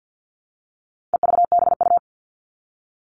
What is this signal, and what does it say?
Morse code, heard as a keyed tone, about 50 wpm, 700 Hz. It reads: EVBU
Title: All Cut Numbers - 4-Digit 50wpm